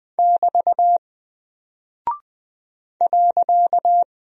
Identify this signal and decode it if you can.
Morse code, heard as a keyed tone, about 20 wpm, 700 Hz. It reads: <BT> .
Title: Single Character (Letter:Number:Punctuation:Prosign) Alt 20wpm